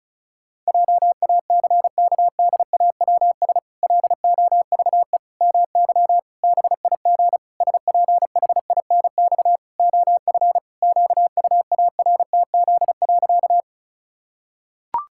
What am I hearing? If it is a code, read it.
Morse code, heard as a keyed tone, about 34 wpm, 700 Hz. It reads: JACKDAWS LOVE MY BIG SPHINX OF QUARTZ.
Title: Pangram Challenge v2 - 35wpm